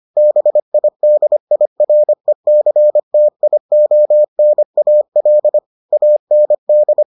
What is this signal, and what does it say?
Morse code, heard as a keyed tone, about 25 wpm, 600 Hz. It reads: BIDIRECTIONAL AND